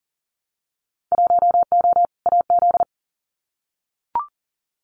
Morse code, heard as a keyed tone, about 40 wpm, 700 Hz. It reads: 1O AZ
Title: Winter Field Day Exchanges 40wpm